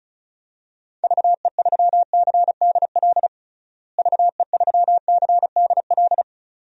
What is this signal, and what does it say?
Morse code, heard as a keyed tone, about 36 wpm, 700 Hz. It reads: VE3CDL VE3CDL
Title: DXCC-100 Call Signs - 6 Characters - Repeated 2x - 35wpm